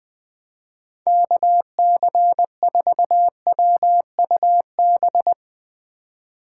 Morse code, heard as a keyed tone, about 22 wpm, 700 Hz. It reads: KC4WUB
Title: Speed Racing - USA 2x3 Call Signs 20wpm